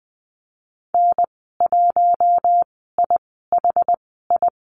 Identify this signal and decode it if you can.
Morse code, heard as a keyed tone, about 20 wpm, spaced as for 14 wpm, 700 Hz. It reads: N1IHI
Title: 062 Lesson 11g - Letter h - Call Signs 20wpm